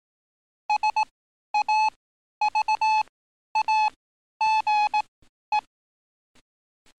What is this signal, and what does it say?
Morse code, heard as a keyed tone, about 18 wpm, spaced as for 11 wpm, 860 Hz. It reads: SAVAGE
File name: training / NAVPERS 11049to11060 / NAVPERS 11051 RA RB